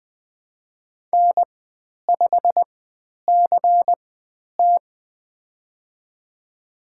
Morse code, heard as a keyed tone, about 20 wpm, spaced as for 10 wpm, 700 Hz. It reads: N5CT
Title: 095 Lesson 16g - Letter c - Call Signs 20wpm